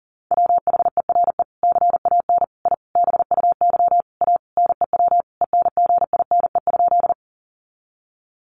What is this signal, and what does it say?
Morse code, heard as a keyed tone, about 40 wpm, 700 Hz. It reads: WHERE CAN I BUY A NEW ENGINE?
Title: Sentences from Top 1500 Words 40wpm